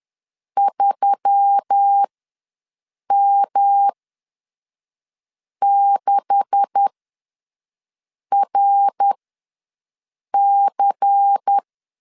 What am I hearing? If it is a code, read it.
Morse code, heard as a keyed tone, about 10 wpm, spaced as for 5 wpm, 800 Hz. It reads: 3M6RC
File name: LEZIONI / 10 SIGNS / 10D COMA ws 10wpm 4314